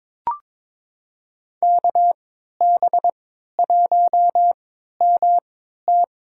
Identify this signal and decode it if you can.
Morse code, heard as a keyed tone, about 22 wpm, spaced as for 13 wpm, 700 Hz. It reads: KB1MT
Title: USA Call Signs - 3x Character Spacing - 22wpm